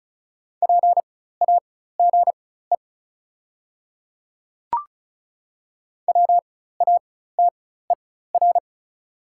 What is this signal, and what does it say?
Morse code, heard as a keyed tone, about 34 wpm, spaced as for 16 wpm, 700 Hz. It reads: PAGE WATER
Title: 4x Character Spacing - Top 500 Words - 35wpm